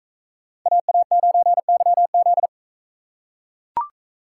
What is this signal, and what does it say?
Morse code, heard as a keyed tone, about 40 wpm, 700 Hz. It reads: AA9YZ